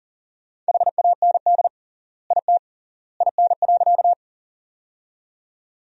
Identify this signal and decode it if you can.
Morse code, heard as a keyed tone, about 40 wpm, 700 Hz. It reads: HAND IT IN.
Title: 3x Word-Spacing - Sentences from Top 200 Words - 40wpm